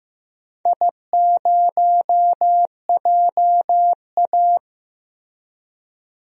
Morse code, heard as a keyed tone, about 16 wpm, 700 Hz. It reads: I0JA
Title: DXCC-100 Call Signs - 4 Characters - Repeated 3x - 15wpm